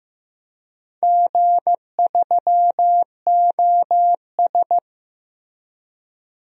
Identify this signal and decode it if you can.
Morse code, heard as a keyed tone, about 15 wpm, 700 Hz. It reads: G3OS
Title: DXCC-100 Call Signs - 4 Characters - Repeated 2x - 15wpm